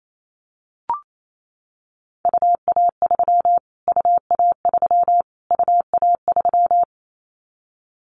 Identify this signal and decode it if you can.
Morse code, heard as a keyed tone, about 28 wpm, 700 Hz. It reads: UA3 UA3 UA3
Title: Callsign Prefixes for Top 25 DXCC - Repeated 3x - 28wpm